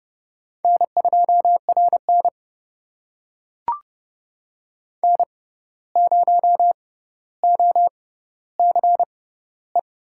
Morse code, heard as a keyed tone, about 31 wpm, spaced as for 15 wpm, 700 Hz. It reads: N2RN N0OCE
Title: USA Call Signs - 6x Character Spacing - 30wpm